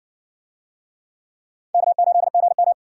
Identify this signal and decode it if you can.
Morse code, heard as a keyed tone, about 60 wpm, 700 Hz. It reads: K8GM